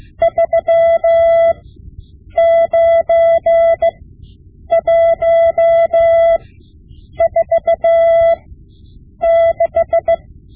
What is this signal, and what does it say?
Morse code, heard as a keyed tone, about 16 wpm, spaced as for 8 wpm, 670 Hz. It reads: 39146
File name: Morse Lessons / Morse Lessons / Tape-5B